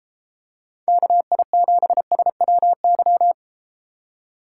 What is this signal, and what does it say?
Morse code, heard as a keyed tone, about 32 wpm, 700 Hz. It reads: KI7SWY